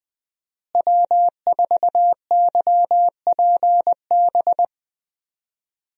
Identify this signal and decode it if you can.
Morse code, heard as a keyed tone, about 20 wpm, 700 Hz. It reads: W4YPB
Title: USA Call Signs - 5 Characters - Repeated 2x - 20wpm